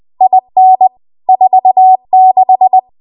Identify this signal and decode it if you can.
Morse code, heard as a keyed tone, about 20 wpm, 750 Hz. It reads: IN 46
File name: Archive / 20 WPM / 20WPM